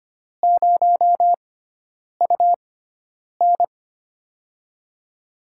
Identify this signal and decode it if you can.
Morse code, heard as a keyed tone, about 25 wpm, spaced as for 9 wpm, 700 Hz. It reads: 0UN
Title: USA Call Signs - 4 Characters - 6x Character Spacing - 25wpm